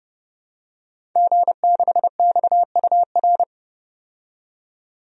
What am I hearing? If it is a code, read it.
Morse code, heard as a keyed tone, about 30 wpm, 700 Hz. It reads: G6XUR
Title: DXCC-100 Call Signs - 5 Characters - 30wpm